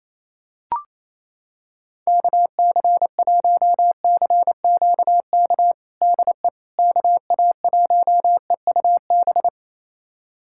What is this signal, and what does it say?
Morse code, heard as a keyed tone, about 27 wpm, 700 Hz. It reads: KC1CQK DE KA1EUB
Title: USA Callsign DE Callsign 28wpm